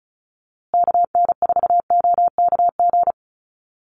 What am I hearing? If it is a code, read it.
Morse code, heard as a keyed tone, about 34 wpm, 700 Hz. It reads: KN4OKG
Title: USA Call Signs Repeated 3x 35wpm